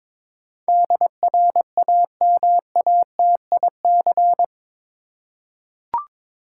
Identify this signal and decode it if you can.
Morse code, heard as a keyed tone, about 21 wpm, 700 Hz. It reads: DRAMATIC
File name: practice / 22wpm / Eight Letter Words - 22wpm